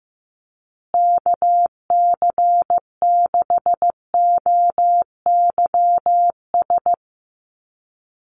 Morse code, heard as a keyed tone, about 15 wpm, 700 Hz. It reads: KC6OYS
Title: USA Call Signs Repeated 2x 15wpm